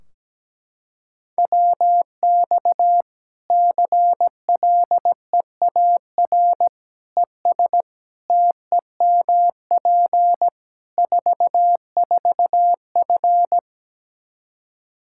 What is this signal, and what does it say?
Morse code, heard as a keyed tone, about 17 wpm, 700 Hz. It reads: WX CLEAR ES TEMP 44F
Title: Practice QSO - Exchange 2 - Info - 17wpm